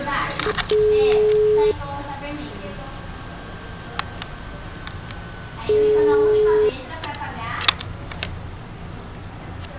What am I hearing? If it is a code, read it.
Ringback tone (Europe, 425 Hz), with speech.
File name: out- -2009- - - .49746